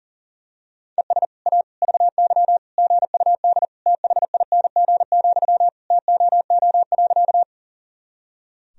Morse code, heard as a keyed tone, about 40 wpm, 700 Hz. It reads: ES A VY GUD THING, TOO.
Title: Sentences from Top 200 Words and Abbreviations 40wpm